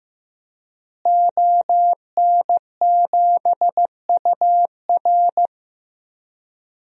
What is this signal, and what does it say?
Morse code, heard as a keyed tone, about 16 wpm, 700 Hz. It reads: ON7UR
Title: DXCC-100 Call Signs - 5 Characters - Repeated 3x - 15wpm